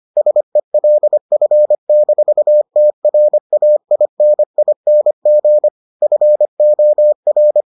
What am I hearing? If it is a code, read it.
Morse code, heard as a keyed tone, about 25 wpm, 600 Hz. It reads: SELF-TRAINING FOR